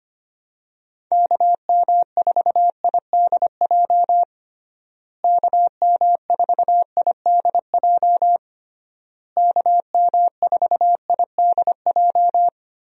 Morse code, heard as a keyed tone, about 25 wpm, 700 Hz. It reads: KM4IDJ KM4IDJ KM4IDJ
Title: USA Call Signs - 6 Characters - Repeated 3x - 25wpm